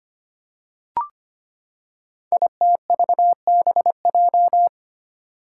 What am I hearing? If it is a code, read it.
Morse code, heard as a keyed tone, about 25 wpm, 700 Hz. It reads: ITVBJ
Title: Code Groups - 5 Letters - 25wpm